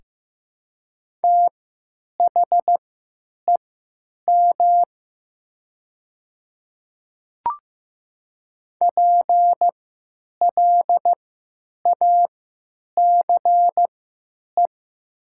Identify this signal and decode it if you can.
Morse code, heard as a keyed tone, about 15 wpm, spaced as for 9 wpm, 700 Hz. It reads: THEM PLACE
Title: 3x Character Spacing - Top 300 Words - 15wpm